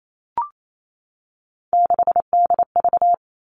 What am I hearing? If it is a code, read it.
Morse code, heard as a keyed tone, about 28 wpm, 700 Hz. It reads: 6DV
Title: All Cut Numbers - 3-Digit 28wpm